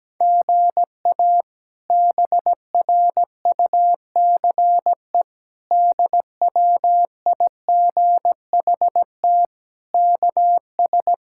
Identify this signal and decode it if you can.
Morse code, heard as a keyed tone, about 17 wpm, 700 Hz. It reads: GA BRUCE DWIGHT KS
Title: K1USN SST QSO Practice - 17wpm